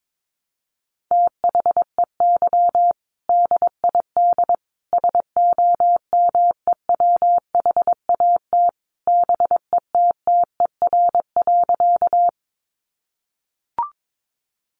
Morse code, heard as a keyed tone, about 22 wpm, 700 Hz. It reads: THEY DID SOMEWHAT BETTER.
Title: Sentences from Top 1300 Words 22wpm